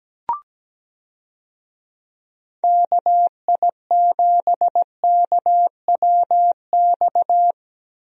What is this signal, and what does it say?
Morse code, heard as a keyed tone, about 17 wpm, 700 Hz. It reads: KI7KWX
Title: USA Call Signs - 6 Characters - Repeated 3x - 17wpm